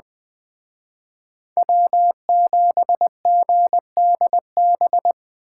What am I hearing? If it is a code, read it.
Morse code, heard as a keyed tone, about 25 wpm, 700 Hz. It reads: W7GDB
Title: Speed Racing - USA Call Signs 20wpm